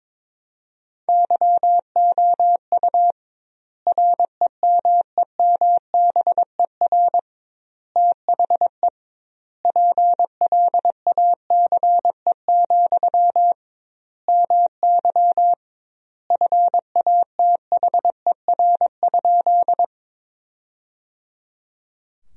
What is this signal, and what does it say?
Morse code, heard as a keyed tone, about 22 wpm, 700 Hz. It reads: YOU REMEMBER THE PLACE, MY FATHER?
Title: 2x Word-Spacing - Sentences from Top 400 Words - 22wpm